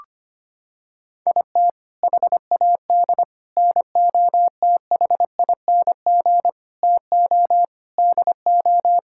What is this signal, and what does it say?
Morse code, heard as a keyed tone, about 25 wpm, 700 Hz. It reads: IT HAD NOTHING TO DO
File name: practice / 25wpm / Sentences with 4-Word Phrases - 25wpm